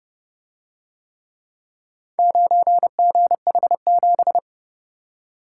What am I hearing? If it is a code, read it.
Morse code, heard as a keyed tone, about 29 wpm, 700 Hz. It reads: 9GH7